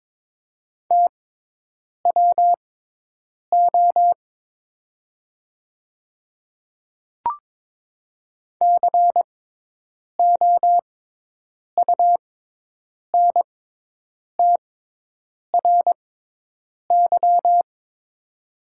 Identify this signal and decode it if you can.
Morse code, heard as a keyed tone, about 22 wpm, spaced as for 8 wpm, 700 Hz. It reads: TWO COUNTRY
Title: 6x Character Spacing - Top 300 Words - 22wpm